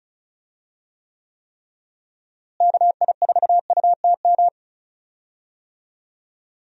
Morse code, heard as a keyed tone, about 34 wpm, 700 Hz. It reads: KI4UTM
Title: USA Call Signs - 6 Characters - Repeated 3x - 35wpm